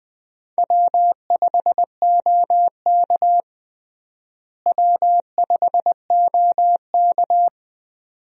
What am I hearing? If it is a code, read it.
Morse code, heard as a keyed tone, about 21 wpm, 700 Hz. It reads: W5OK W5OK
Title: USA Call Signs - 4 Characters - Repeated 2x - 20wpm